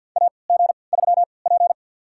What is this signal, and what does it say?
Morse code, heard as a keyed tone, about 52 wpm, spaced as for 28 wpm, 700 Hz. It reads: AG3P
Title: USA Call Signs - 4 Characters - 3x Character Spacing - 50wpm